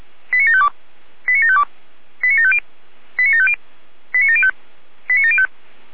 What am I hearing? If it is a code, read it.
Five-tone selective call (ZVEI), heard as quick runs of five tones, tones of about 70 ms. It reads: 89752 89752 89750 89750 89705 89705